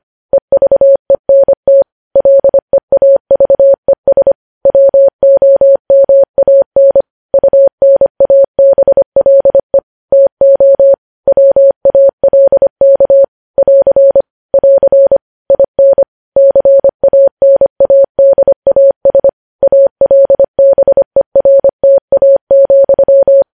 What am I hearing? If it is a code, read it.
Morse code, heard as a keyed tone, about 25 wpm, 550 Hz. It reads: EVENT LEAVES WOMAN UNABLE TO WALK <AR> <AR> IN CANADAS ALBERTA,
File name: podcast / News headlines-1x-25x25